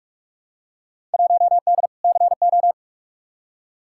Morse code, heard as a keyed tone, about 43 wpm, 700 Hz. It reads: 1D CO
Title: Field Day Exchanges - 45wpm